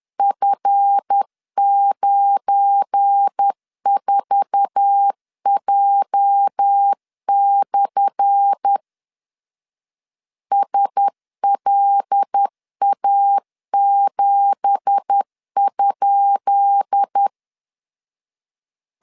Morse code, heard as a keyed tone, about 10 wpm, 800 Hz. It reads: F94J/ SLA7?
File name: LEZIONI / 10 SIGNS / 10D COMA 10wpm 3013